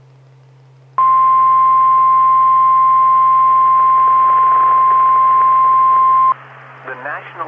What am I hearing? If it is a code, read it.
Emergency Alert System (SAME) tones (the 1050 Hz attention tone), with speech. It is heard from 1.0 to 6.4 s.